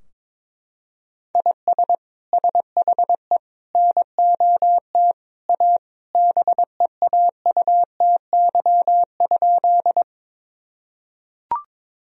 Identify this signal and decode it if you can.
Morse code, heard as a keyed tone, about 21 wpm, 700 Hz. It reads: IS SHE NOT A BEAUTY?
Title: Sentences from Top 1000 Words - Review Part 2 - 22wpm